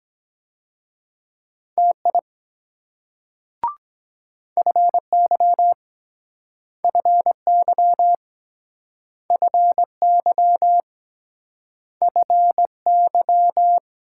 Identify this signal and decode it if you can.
Morse code, heard as a keyed tone, about 23 wpm, 700 Hz. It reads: TI FY FY FY FY